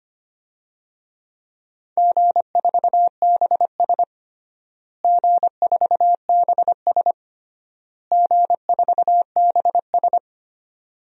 Morse code, heard as a keyed tone, about 25 wpm, 700 Hz. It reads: G4BS G4BS G4BS
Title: DXCC-100 Call Signs - 4 Characters - Repeated 3x - 25wpm